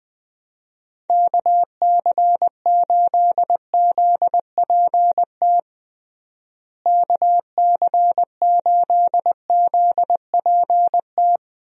Morse code, heard as a keyed tone, about 20 wpm, 700 Hz. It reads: KC8ZPT KC8ZPT
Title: USA Call Signs - 6 Characters - Repeated 3x - 20wpm